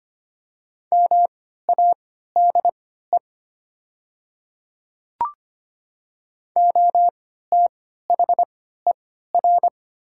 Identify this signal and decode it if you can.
Morse code, heard as a keyed tone, about 25 wpm, spaced as for 14 wpm, 700 Hz. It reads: MADE OTHER
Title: 3x Character Spacing - Top 200 Words - 25wpm